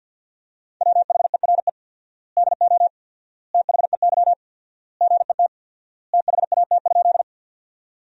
Morse code, heard as a keyed tone, about 50 wpm, 700 Hz. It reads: WHERE DO THEY GET THAT?
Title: 4x Word-Spacing - Sentences from Top 200 Words - 50wpm